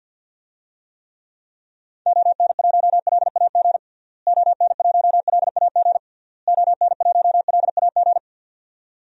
Morse code, heard as a keyed tone, about 52 wpm, 700 Hz. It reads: ON1LAG ON1LAG ON1LAG
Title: DXCC-100 Call Signs - 6 Characters - Repeated 3x - 50wpm